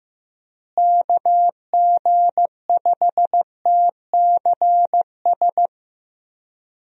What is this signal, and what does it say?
Morse code, heard as a keyed tone, about 16 wpm, 700 Hz. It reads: KG5TCS
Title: USA Call Signs - 6 Characters - Repeated 2x - 15wpm